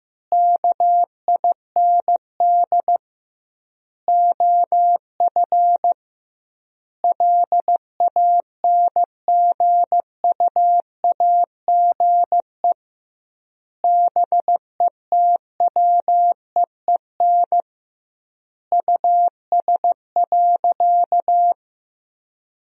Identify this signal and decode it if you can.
Morse code, heard as a keyed tone, about 15 wpm, 700 Hz. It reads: KIND OF LANGUAGE BETWEEN US.
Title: 2x Word-Spacing - Sentences from Top 500 Words - 15wpm